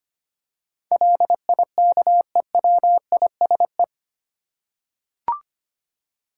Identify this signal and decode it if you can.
Morse code, heard as a keyed tone, about 25 wpm, 700 Hz. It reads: LIKEWISE
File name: 8 Letter Words - First 3 Letters Given - 25wpm